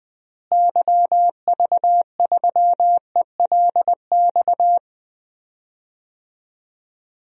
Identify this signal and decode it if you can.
Morse code, heard as a keyed tone, about 20 wpm, 700 Hz. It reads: YV3ELX
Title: DXCC-100 Call Signs - 6 Characters - Repeated 2x - 20wpm